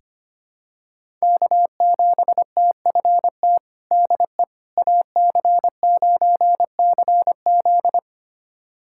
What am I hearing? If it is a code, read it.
Morse code, heard as a keyed tone, about 25 wpm, 700 Hz. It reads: K7TFT DE AC9CZ
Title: USA Callsign DE Callsign - 5 Characters - 25wpm